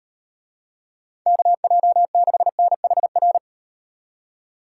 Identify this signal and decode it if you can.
Morse code, heard as a keyed tone, about 38 wpm, 700 Hz. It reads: KJ6NHR